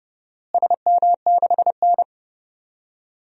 Morse code, heard as a keyed tone, about 30 wpm, 700 Hz. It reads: SM6N
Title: DXCC-100 Call Signs - 4 Characters - Repeated 3x - 30wpm